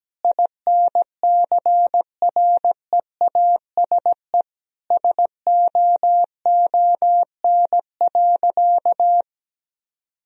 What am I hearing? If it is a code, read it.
Morse code, heard as a keyed tone, about 17 wpm, 700 Hz. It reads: INCREASE SOON.
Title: Sentences from Top 1000 Words - Review Part 4 - 17wpm